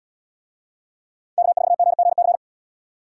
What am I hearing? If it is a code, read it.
Morse code, heard as a keyed tone, about 68 wpm, 700 Hz. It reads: K4GGC